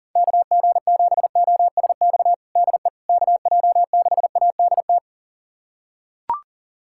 Morse code, heard as a keyed tone, about 40 wpm, 700 Hz. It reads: KG7OSX DE KJ6ADT
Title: USA Callsign DE Callsign 40wpm